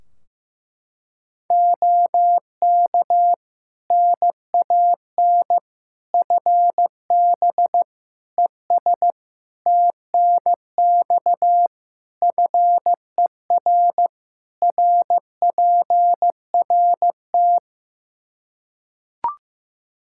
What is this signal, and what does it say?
Morse code, heard as a keyed tone, about 15 wpm, 700 Hz. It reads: OK NAN FB ES TNX FER RPRT
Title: Practice QSO - Exchange 2 - Info - 15wpm